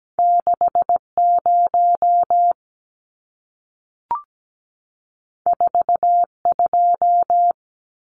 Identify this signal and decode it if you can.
Morse code, heard as a keyed tone, about 17 wpm, 700 Hz. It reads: 60 42
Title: Numbers - 2-Digit 17wpm